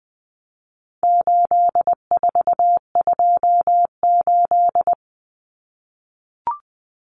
Morse code, heard as a keyed tone, about 20 wpm, 700 Hz. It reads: 8428
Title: Numbers - 4-Digit 20wpm